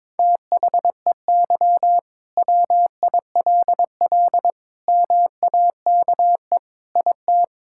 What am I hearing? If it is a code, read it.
Morse code, heard as a keyed tone, about 22 wpm, 700 Hz. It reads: THEY WILL MAKE IT
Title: Sentences from Top 1600 Words 22wpm